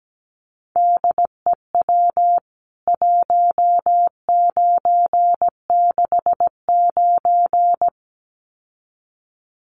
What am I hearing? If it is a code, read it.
Morse code, heard as a keyed tone, about 17 wpm, 700 Hz. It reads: DEW 1969
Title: CWT Exchange 17wpm